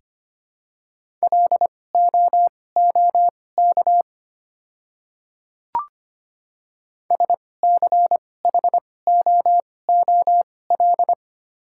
Morse code, heard as a keyed tone, about 25 wpm, spaced as for 18 wpm, 700 Hz. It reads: LOOK SCHOOL
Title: 2x Character Spacing - Top 300 Words - 25wpm